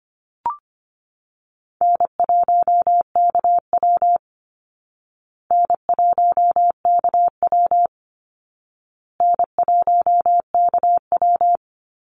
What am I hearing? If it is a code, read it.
Morse code, heard as a keyed tone, about 25 wpm, 700 Hz. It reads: N1KW N1KW N1KW
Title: Call Signs Repeated 2x - 25wpm